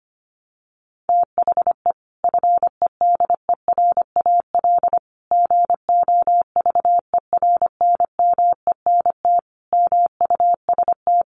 Morse code, heard as a keyed tone, about 25 wpm, 700 Hz. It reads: THE FEDERAL GOVERNMENT MUST
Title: Sentences from Top 1100 Words 25wpm